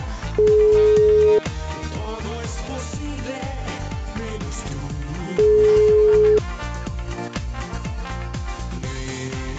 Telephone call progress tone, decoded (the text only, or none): ringback tone